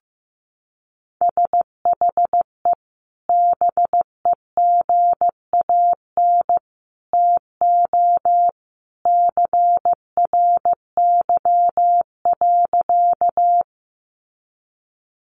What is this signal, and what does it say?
Morse code, heard as a keyed tone, about 15 wpm, 700 Hz. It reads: SHE BEGAN TO CRY.
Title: Sentences from Top 500 Words 15wpm